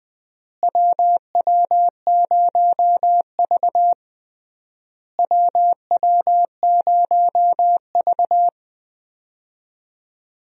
Morse code, heard as a keyed tone, about 20 wpm, 700 Hz. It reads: WW0V WW0V
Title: USA Call Signs - 4 Characters - Repeated 3x - 20wpm